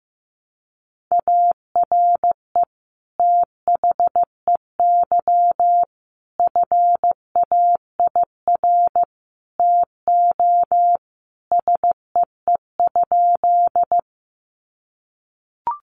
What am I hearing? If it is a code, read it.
Morse code, heard as a keyed tone, about 15 wpm, 700 Hz. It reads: ARE THEY FAIR TO SEE?
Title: Sentences from Top 1300 Words 15wpm